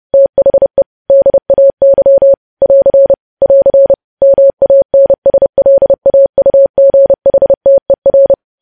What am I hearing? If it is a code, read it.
Morse code, heard as a keyed tone, about 30 wpm, 550 Hz. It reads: THE DAY <AR> <AR> MANSLAUGHTER
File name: podcast / News headlines-1x-30x30